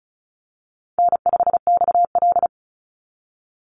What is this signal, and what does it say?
Morse code, heard as a keyed tone, about 34 wpm, 700 Hz. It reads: N5XL
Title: NA SOTA Chaser Callsigns 35wpm